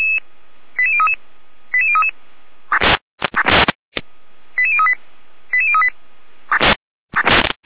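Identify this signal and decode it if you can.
Five-tone selective call (ZVEI), heard as quick runs of five tones, tones of about 70 ms. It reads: 88033 88033 88038 88038
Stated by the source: ZVEI five-tone call received on SDR.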